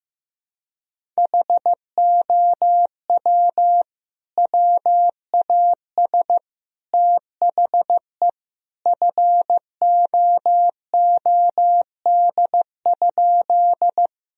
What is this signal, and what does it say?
Morse code, heard as a keyed tone, about 15 wpm, 700 Hz. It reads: HOW WAS THE FOOD?
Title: Sentences from Top 1000 Words - Review Part 1 - 15wpm